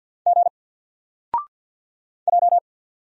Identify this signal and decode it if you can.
Morse code, heard as a keyed tone, about 50 wpm, 700 Hz. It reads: G J
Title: Single Letter-Number Alt 50wpm